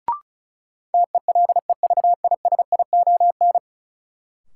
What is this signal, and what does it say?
Morse code, heard as a keyed tone, about 36 wpm, 700 Hz. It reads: TELEVISION
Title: Sentences from Top 1000 Words - Review Part 4 - New Word Drill - 35wpm